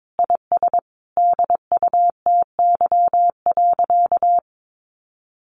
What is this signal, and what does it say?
Morse code, heard as a keyed tone, about 22 wpm, 700 Hz. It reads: IS DUTY.
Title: Sentences from Top 1500 Words 22wpm